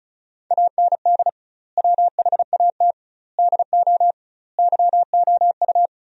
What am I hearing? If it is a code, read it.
Morse code, heard as a keyed tone, about 36 wpm, 700 Hz. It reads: AND WHAT DO YOU
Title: 2x Word-Spacing - Sentences from Top 200 Words - 35wpm